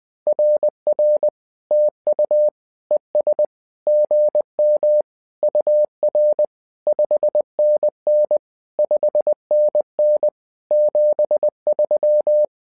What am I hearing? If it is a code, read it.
Morse code, heard as a keyed tone, about 20 wpm, 600 Hz. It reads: RR TU ES GM UR 5NN 5NN 73
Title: SOTA QSO Practice 20wpm